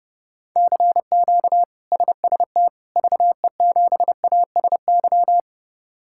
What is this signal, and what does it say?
Morse code, heard as a keyed tone, about 30 wpm, 700 Hz. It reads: CQ SST VE7ASY
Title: K1USN SST QSO Practice - 30wpm